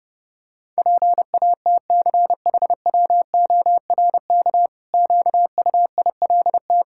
Morse code, heard as a keyed tone, about 30 wpm, 700 Hz. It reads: PATCHWORK QUILT
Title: Pangram Challenge v2 - 30wpm